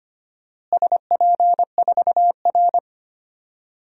Morse code, heard as a keyed tone, about 25 wpm, 700 Hz. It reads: SP4R
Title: DXCC-100 Call Signs - 4 Characters - Repeated 2x - 25wpm